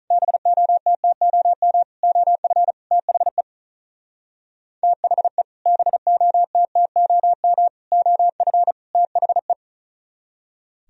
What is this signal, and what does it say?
Morse code, heard as a keyed tone, about 44 wpm, 700 Hz. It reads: BOTTOM OF THE THE BOTTOM OF THE